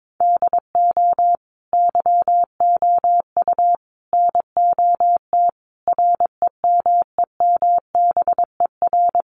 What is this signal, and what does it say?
Morse code, heard as a keyed tone, about 22 wpm, 700 Hz. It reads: DO YOU NOT REMEMBER
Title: Sentences from Top 600 Words 22wpm